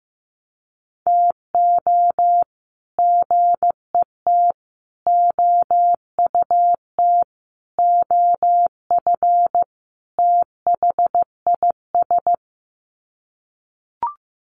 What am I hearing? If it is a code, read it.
Morse code, heard as a keyed tone, about 15 wpm, 700 Hz. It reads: TO GET OUT OF THIS
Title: Sets of 5 Words Encore2 15wpm